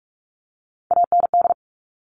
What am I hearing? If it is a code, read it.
Morse code, heard as a keyed tone, about 46 wpm, 700 Hz. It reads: AND
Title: Top Three Letter Combinations in Words 45wpm